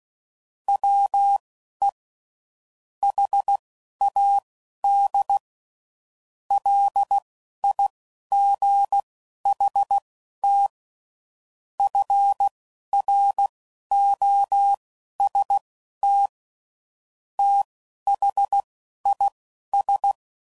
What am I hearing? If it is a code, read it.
Morse code, heard as a keyed tone, about 16 wpm, spaced as for 11 wpm, 800 Hz. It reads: WE HAD LIGHT FROST THIS
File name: CW Code Groups @10WPM / Weather Sets / Weather 4 28